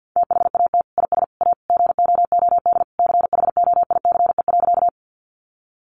Morse code, heard as a keyed tone, about 50 wpm, 700 Hz. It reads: THAT IS A GOOD CHOICE.
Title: Sentences from Top 900 Words 50wpm